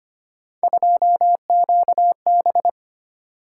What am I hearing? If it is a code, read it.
Morse code, heard as a keyed tone, about 25 wpm, 700 Hz. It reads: 2QB
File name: Three Letter-Number Combinations - 25wpm